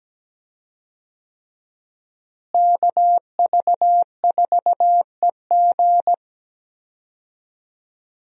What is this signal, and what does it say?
Morse code, heard as a keyed tone, about 17 wpm, 700 Hz. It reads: KV4EG
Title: USA Call Signs - 5 Characters - Repeated 3x - 17wpm